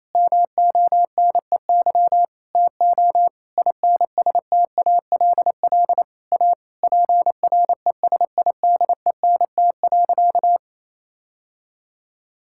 Morse code, heard as a keyed tone, about 28 wpm, 700 Hz. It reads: MONEY TO INSTALL A PRESIDENT.
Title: Sentences from Top 1600 Words 28wpm